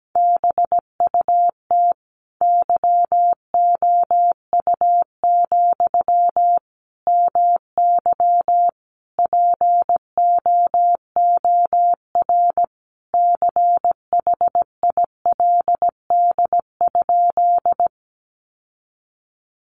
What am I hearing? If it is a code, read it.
Morse code, heard as a keyed tone, about 17 wpm, 700 Hz. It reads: BUT YOU, MY POOR CHILD?
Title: Sentences from Top 1000 Words 17wpm